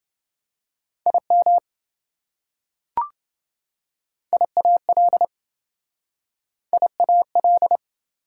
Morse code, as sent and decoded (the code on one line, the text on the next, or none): .. -- / .. .- .-.. / .. .- .-..
IM IAL IAL